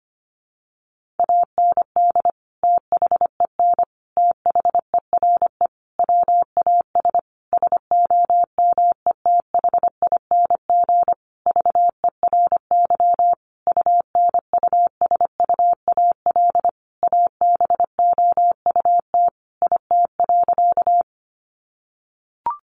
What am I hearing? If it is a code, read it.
Morse code, heard as a keyed tone, about 25 wpm, 700 Hz. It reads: AND THEN THERE WAS SOMETHING VERY UNUSUAL ABOUT IT.
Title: Sentences from Top 1400 Words 25wpm